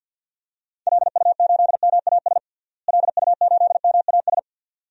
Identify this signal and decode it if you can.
Morse code, heard as a keyed tone, about 53 wpm, 700 Hz. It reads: LU8MAS LU8MAS
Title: DXCC-100 Call Signs - 6 Characters - Repeated 2x - 50wpm